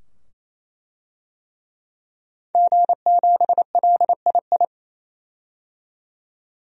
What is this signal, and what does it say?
Morse code, heard as a keyed tone, about 28 wpm, 700 Hz. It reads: G7LII
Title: DXCC-100 Call Signs - 5 Characters - 28wpm